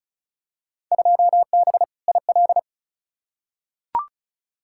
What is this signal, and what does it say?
Morse code, heard as a keyed tone, about 36 wpm, 700 Hz. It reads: 2B IL
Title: Field Day Exchanges Encore - 35wpm